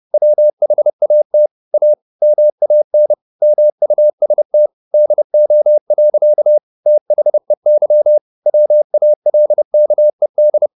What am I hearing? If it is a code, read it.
Morse code, heard as a keyed tone, about 30 wpm, 600 Hz. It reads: WHAT A MAN MUST DO. THEY WALKED
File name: the old man and the sea / the old man and the sea 1 30wpm